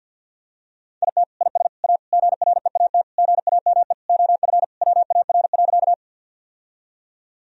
Morse code, heard as a keyed tone, about 52 wpm, 700 Hz. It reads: IT IS A GREAT GAME OF WAR.
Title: Sentences from Top 1000 Words - Review Part 3 - 50wpm